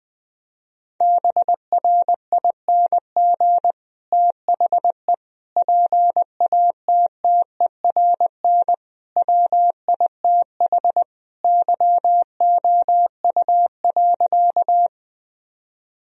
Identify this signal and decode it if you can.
Morse code, heard as a keyed tone, about 20 wpm, 700 Hz. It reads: BRING THE PATTERN WITH YOU.
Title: Sentences from Top 1000 Words - Review Part 2 - 20wpm